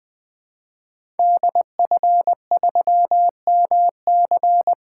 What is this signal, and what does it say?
Morse code, heard as a keyed tone, about 20 wpm, 700 Hz. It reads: DF3MC
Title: 2023 SOTA Activator Call Signs - 20wpm